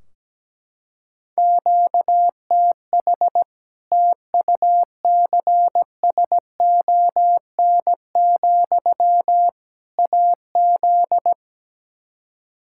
Morse code, heard as a keyed tone, about 17 wpm, 700 Hz. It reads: QTH TUCSON, AZ
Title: Practice QSO - Exchange 1 - Beginning - 17wpm